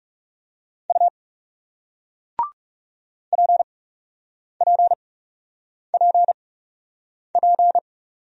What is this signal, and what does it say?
Morse code, heard as a keyed tone, about 38 wpm, spaced as for 8 wpm, 700 Hz. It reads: U PPPP